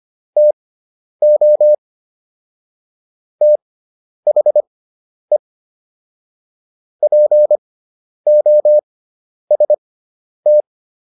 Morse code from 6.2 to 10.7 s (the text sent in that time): POST